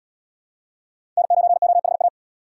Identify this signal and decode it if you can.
Morse code, heard as a keyed tone, about 68 wpm, 700 Hz. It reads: N9OHA